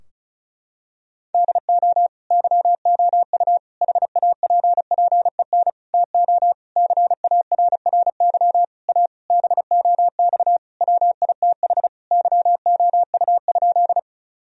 Morse code, heard as a keyed tone, about 34 wpm, 700 Hz. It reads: DO YOU HAPPEN TO CARRY A BOX WITH YOU?
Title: Sentences from Top 1000 Words - Review Part 2 - 35wpm